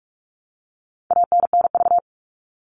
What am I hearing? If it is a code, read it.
Morse code, heard as a keyed tone, about 44 wpm, 700 Hz. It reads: ANNV
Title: All Cut Numbers - 4-Digit 45wpm